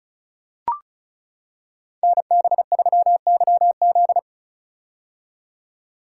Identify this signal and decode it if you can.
Morse code, heard as a keyed tone, about 34 wpm, 700 Hz. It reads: NB3YZ
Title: Code Groups - 5 Letter:Numbers - 35wpm